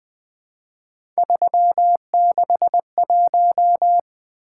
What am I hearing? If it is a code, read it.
Morse code, heard as a keyed tone, about 20 wpm, 700 Hz. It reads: 361
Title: Three Letter-Number Combinations - 20wpm